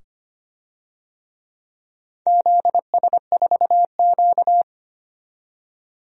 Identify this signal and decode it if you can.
Morse code, heard as a keyed tone, about 25 wpm, 700 Hz. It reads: ZS4Q
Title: DXCC-100 Call Signs - 4 Characters - 25wpm